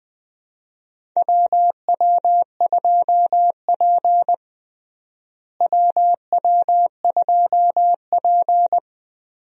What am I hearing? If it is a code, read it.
Morse code, heard as a keyed tone, about 20 wpm, 700 Hz. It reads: WW2P WW2P
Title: USA Call Signs - 4 Characters - Repeated 2x - 20wpm